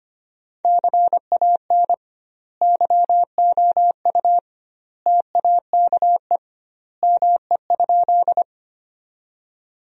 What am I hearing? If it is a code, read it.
Morse code, heard as a keyed tone, about 25 wpm, 700 Hz. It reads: CAN YOU TAKE ME?
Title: 2x Word-Spacing - Sentences from Top 200 Words - 25wpm